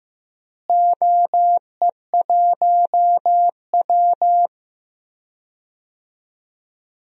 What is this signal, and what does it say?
Morse code, heard as a keyed tone, about 15 wpm, 700 Hz. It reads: OE1W
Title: DXCC-100 Call Signs - 4 Characters - Repeated 2x - 15wpm